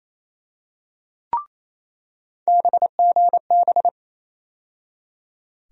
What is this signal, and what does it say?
Morse code, heard as a keyed tone, about 28 wpm, 700 Hz. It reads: BGB
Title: USA Call Sign Postfixes - 3 Letters 28wpm